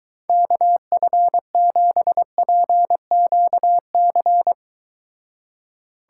Morse code, heard as a keyed tone, about 23 wpm, 700 Hz. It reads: KF7PQC